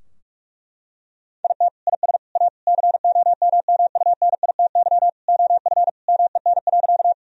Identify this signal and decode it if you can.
Morse code, heard as a keyed tone, about 45 wpm, 700 Hz. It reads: IT IS A COMMUNITY OF MEN.
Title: Sentences from Top 1000 Words - Review Part 4 - 45wpm